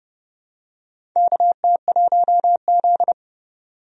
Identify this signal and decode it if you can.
Morse code, heard as a keyed tone, about 30 wpm, 700 Hz. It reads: KT1Z
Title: USA Call Signs - 4 Characters - 30wpm